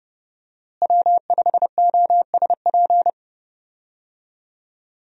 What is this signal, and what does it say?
Morse code, heard as a keyed tone, about 30 wpm, 700 Hz. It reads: W5OSP
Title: USA Call Signs - 5 Characters - Repeated 3x - 30wpm